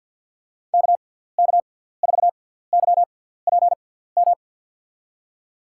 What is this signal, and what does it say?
Morse code, heard as a keyed tone, about 52 wpm, spaced as for 17 wpm, 700 Hz. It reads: KK4YPM
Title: USA Call Signs - 6x Character Spacing - 50wpm